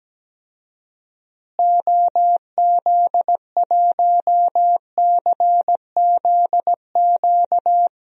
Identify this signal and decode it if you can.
Morse code, heard as a keyed tone, about 17 wpm, 700 Hz. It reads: OZ1CZQ
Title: DXCC-100 Call Signs - 6 Characters - Repeated 2x - 17wpm